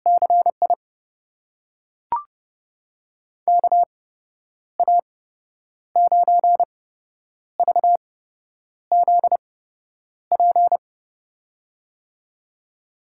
Morse code, heard as a keyed tone, about 31 wpm, spaced as for 9 wpm, 700 Hz. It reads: CI KA9VZP
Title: USA Call Signs - 6 Characters - 8x Character Spacing - 30wpm